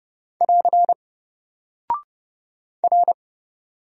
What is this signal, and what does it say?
Morse code, heard as a keyed tone, about 30 wpm, 700 Hz. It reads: <AR> R
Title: Single Character (Letter:Number:Punctuation:Prosign) Alt 30wpm